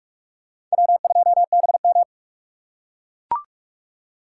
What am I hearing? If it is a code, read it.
Morse code, heard as a keyed tone, about 43 wpm, 700 Hz. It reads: W2BM